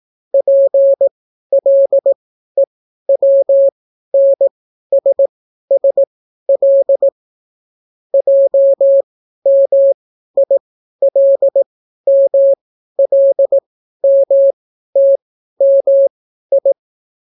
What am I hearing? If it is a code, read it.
Morse code, heard as a keyed tone, about 18 wpm, spaced as for 12 wpm, 550 Hz. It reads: PLEWNSSL JMILMLMTMI